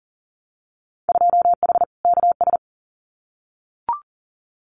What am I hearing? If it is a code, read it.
Morse code, heard as a keyed tone, about 40 wpm, 700 Hz. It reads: 2H KS
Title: Winter Field Day Exchanges Encore 40wpm